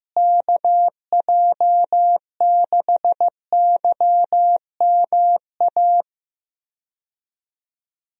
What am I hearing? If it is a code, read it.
Morse code, heard as a keyed tone, about 15 wpm, 700 Hz. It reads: KJ6YMA
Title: USA Call Signs - 6 Characters - Repeated 3x - 15wpm